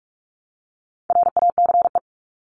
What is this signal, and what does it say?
Morse code, heard as a keyed tone, about 44 wpm, 700 Hz. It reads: RACE
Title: Top 1501-1750 Words 45wpm